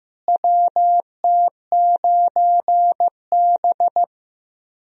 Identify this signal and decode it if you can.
Morse code, heard as a keyed tone, about 15 wpm, 700 Hz. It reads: WT9B
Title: USA Call Signs - 4 Characters - Repeated 3x - 15wpm